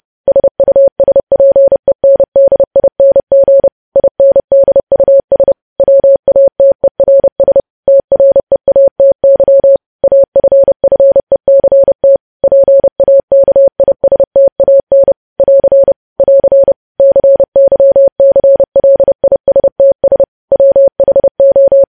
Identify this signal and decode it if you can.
Morse code, heard as a keyed tone, about 30 wpm, 550 Hz. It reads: SUSPENDING INDUS WATERS TREATY AFFECT PAKISTAN <AR> <AR> CYCLISTS WHO
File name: podcast / News headlines-1x-30x30